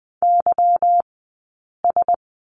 Morse code, heard as a keyed tone, about 20 wpm, spaced as for 8 wpm, 700 Hz. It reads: YS
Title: 144 Lesson 24d - Letter y - Two Characters 20wpm